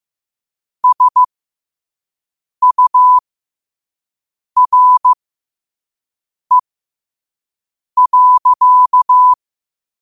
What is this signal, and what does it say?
Morse code, heard as a keyed tone, about 15 wpm, spaced as for 5 wpm, 1000 Hz. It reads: SURE.
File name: qotdfiles / 5wpm